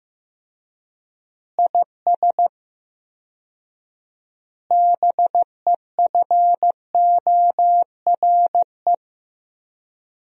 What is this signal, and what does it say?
Morse code, heard as a keyed tone, about 15 wpm, 700 Hz. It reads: IS BEFORE
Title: 4x Word-Spacing - Sentences from Top 200 Words - 15wpm